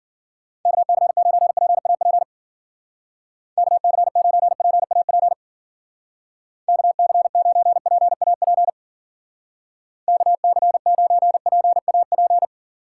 Morse code, heard as a keyed tone, about 50 wpm, 700 Hz. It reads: KC9PAP KC9PAP KC9PAP KC9PAP
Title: Speed Racing - USA 2x3 Call Signs 40wpm